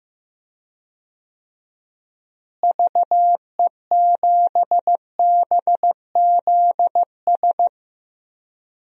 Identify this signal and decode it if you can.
Morse code, heard as a keyed tone, about 15 wpm, 700 Hz. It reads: VE7BZS
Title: DXCC-100 Call Signs - 6 Characters - Repeated 2x - 15wpm